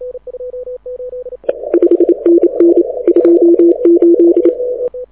Morse code, heard as a keyed tone, about 30 wpm, 340 Hz. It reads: 5NN 28
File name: audio zawody / IARU 2015 CW SN7Q